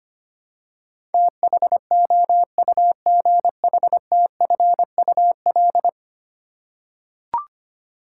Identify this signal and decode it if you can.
Morse code, heard as a keyed tone, about 25 wpm, 700 Hz. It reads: THOUGHTFUL
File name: 10 Letter Words - First 3 Letters Given - 25wpm